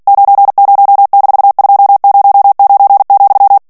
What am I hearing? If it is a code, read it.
Morse code, heard as a keyed tone, about 48 wpm, 800 Hz. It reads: 90-209,
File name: ANGLAIS texte clair 3 vitesse 45 mots minute